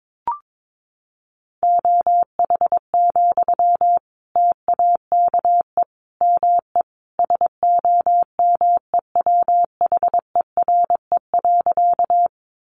Morse code, heard as a keyed tone, about 22 wpm, 700 Hz. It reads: OH, TAKE ME SOMEWHERE.
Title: Sentences from Top 1400 Words 22wpm